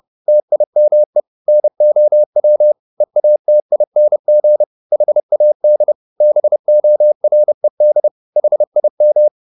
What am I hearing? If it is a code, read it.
Morse code, heard as a keyed tone, about 30 wpm, 600 Hz. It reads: TIME NOW EATING HAD BORED HIM
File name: the old man and the sea / the old man and the sea 1 30wpm